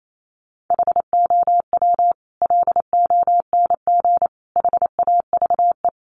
Morse code, heard as a keyed tone, about 28 wpm, 700 Hz. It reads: HOW LONG HAVE
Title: Sets of 5 Words Encore 28wpm